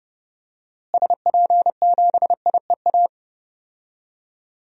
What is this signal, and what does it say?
Morse code, heard as a keyed tone, about 30 wpm, 700 Hz. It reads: SP7IEA
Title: DXCC-100 Call Signs - 6 Characters - Repeated 3x - 30wpm